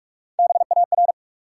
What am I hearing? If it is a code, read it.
Morse code, heard as a keyed tone, about 46 wpm, 700 Hz. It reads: BAR
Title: Three Letter Words - 45wpm